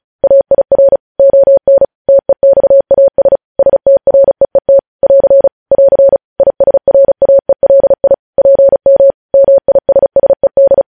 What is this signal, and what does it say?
Morse code, heard as a keyed tone, about 36 wpm, 550 Hz. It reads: AIR ON TEXAS STREET <AR> <AR> ISRAELI PM MISSED